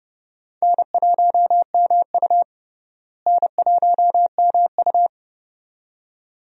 Morse code, heard as a keyed tone, about 32 wpm, 700 Hz. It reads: N1MU N1MU
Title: USA Call Signs - 4 Characters - Repeated 2x - 30wpm